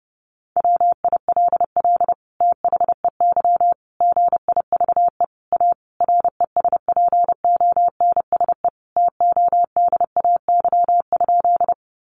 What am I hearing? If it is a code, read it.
Morse code, heard as a keyed tone, about 28 wpm, 700 Hz. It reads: WILL THEY GIVE A RESPONSE TODAY?
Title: Sentences from Top 1400 Words 30wpm